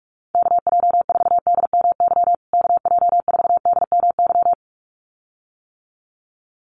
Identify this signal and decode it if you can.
Morse code, heard as a keyed tone, about 44 wpm, 700 Hz. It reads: KJ4DMY KJ4DMY
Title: USA Call Signs Repeated 2x 45wpm